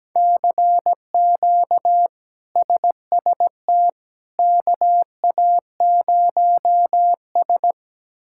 Morse code, heard as a keyed tone, about 17 wpm, 700 Hz. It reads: CQ SST KA0S
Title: K1USN SST QSO Practice - 17wpm